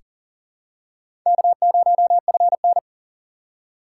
Morse code, heard as a keyed tone, about 40 wpm, 700 Hz. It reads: K0FN
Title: USA Call Signs - 4x Character Spacing - 40wpm